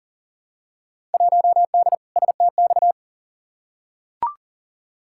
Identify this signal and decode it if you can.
Morse code, heard as a keyed tone, about 40 wpm, 700 Hz. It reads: 1D STX
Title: Field Day Exchanges - 40wpm